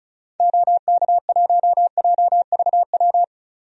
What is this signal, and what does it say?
Morse code, heard as a keyed tone, about 34 wpm, 700 Hz. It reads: OK1JVW
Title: DXCC-100 Call Signs - 6 Characters - 35wpm